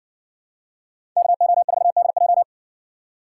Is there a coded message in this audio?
Yes: Morse code from 1.2 to 2.4 s, heard as a keyed tone, about 55 wpm, 700 Hz. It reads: KO4DJ